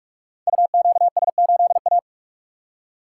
Morse code, heard as a keyed tone, about 44 wpm, 690 Hz. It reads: UQS8A
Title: Code Groups - 5 Letter:Numbers - 45wpm